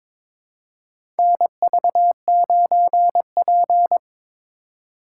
Morse code, heard as a keyed tone, about 22 wpm, 700 Hz. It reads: NV9P
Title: USA Call Signs - 2x Character Spacing - 22wpm